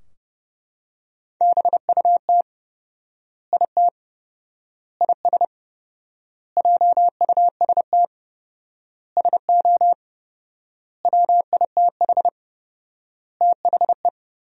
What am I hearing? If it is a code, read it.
Morse code, heard as a keyed tone, about 30 wpm, 700 Hz. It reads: BUT IT IS JUST SO WITH THE
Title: 4x Word-Spacing - Sentences from Top 200 Words - 30wpm